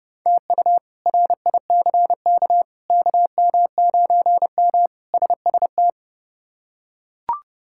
Morse code, heard as a keyed tone, about 30 wpm, 700 Hz. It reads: TU RICK KM9M SST
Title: K1USN SST QSO Practice - 30wpm